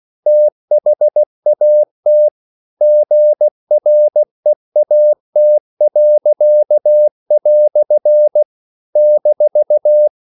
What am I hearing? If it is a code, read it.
Morse code, heard as a keyed tone, about 16 wpm, 600 Hz. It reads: THAT GREAT." -
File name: CW / Practice / EnglishQuotations0116